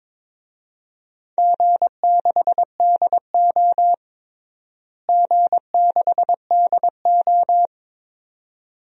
Morse code, heard as a keyed tone, about 21 wpm, 700 Hz. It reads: G6DO G6DO
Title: DXCC-100 Call Signs - 4 Characters - Repeated 2x - 22wpm